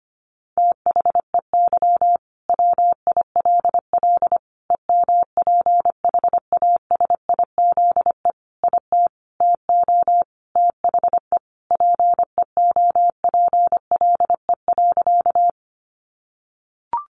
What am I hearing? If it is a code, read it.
Morse code, heard as a keyed tone, about 25 wpm, 700 Hz. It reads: THEY WILL EMPHASIZE IT TO THE PEOPLE.
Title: Sentences from Top 1500 Words 25wpm